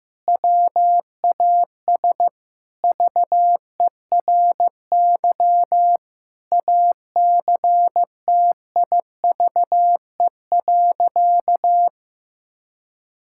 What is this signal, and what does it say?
Morse code, heard as a keyed tone, about 16 wpm, 700 Hz. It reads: WAS VERY ACTIVE.
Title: Sentences from Top 1000 Words - Review Part 6 - 15wpm